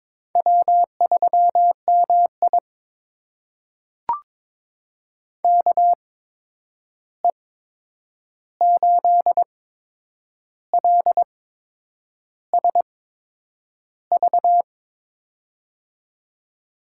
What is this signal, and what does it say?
Morse code, heard as a keyed tone, about 22 wpm, spaced as for 8 wpm, 700 Hz. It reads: W3MI KE8LSV
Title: USA Call Signs - 8x Character Spacing - 22wpm